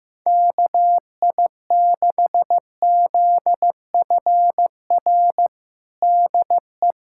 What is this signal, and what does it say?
Morse code, heard as a keyed tone, about 15 wpm, 700 Hz. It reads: KI6ZFR DE
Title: USA Callsign DE Callsign - 6 Characters - 15wpm